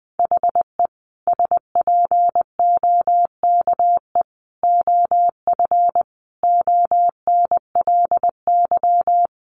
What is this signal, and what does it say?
Morse code, heard as a keyed tone, about 20 wpm, 700 Hz. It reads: HE SPOKE OF ONLY
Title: Sentences from Top 1200 Words 20wpm